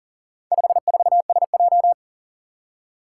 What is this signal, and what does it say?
Morse code, heard as a keyed tone, about 40 wpm, 700 Hz. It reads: 54SJ
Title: Four Challenging Letter-Number Combinations 40wpm